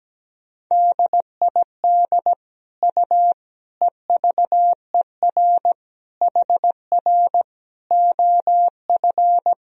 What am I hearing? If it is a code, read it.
Morse code, heard as a keyed tone, about 17 wpm, 700 Hz. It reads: DID U EVER HR OF
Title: Sentences from Top 400 Words and Abbreviations 17wpm